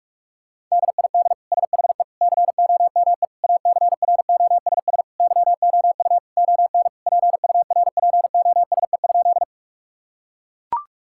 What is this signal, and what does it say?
Morse code, heard as a keyed tone, about 42 wpm, 700 Hz. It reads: DID SHE COME ACROSS YOU ON PURPOSE?
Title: Sentences from Top 1000 Words - Review Part 5 - 45wpm